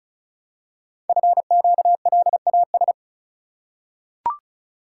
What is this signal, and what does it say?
Morse code, heard as a keyed tone, about 36 wpm, 700 Hz. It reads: FQLAS
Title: Code Groups - 5 Letters - 35wpm